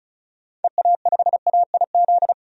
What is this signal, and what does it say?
Morse code, heard as a keyed tone, about 36 wpm, 700 Hz. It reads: EA5AIZ
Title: DXCC-100 Call Signs - 6 Characters - 35wpm